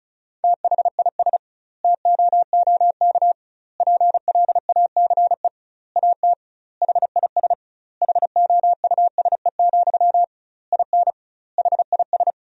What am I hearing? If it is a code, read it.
Morse code, heard as a keyed tone, about 34 wpm, 700 Hz. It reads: THIS TOOK PLACE AT HIS HOUSE, IN HIS
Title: 2x Word-Spacing - Sentences from Top 300 Words - 35wpm